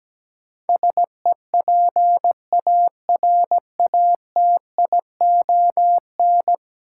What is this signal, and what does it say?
Morse code, heard as a keyed tone, about 17 wpm, 700 Hz. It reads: SEPARATION
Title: Sets of 5 Words - New Word Drill - 17wpm